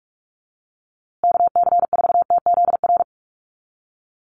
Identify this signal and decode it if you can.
Morse code, heard as a keyed tone, about 44 wpm, 700 Hz. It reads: KC4TZR
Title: USA Call Signs Repeated 3x 45wpm